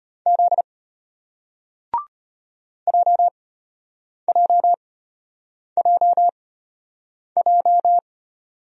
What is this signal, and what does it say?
Morse code, heard as a keyed tone, about 33 wpm, spaced as for 8 wpm, 700 Hz. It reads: Z JJJJ